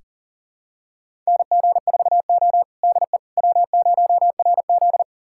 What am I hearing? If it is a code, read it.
Morse code, heard as a keyed tone, about 37 wpm, 700 Hz. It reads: NG4O DE W0RZ
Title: USA Callsign DE Callsign - 4 Characters - 40wpm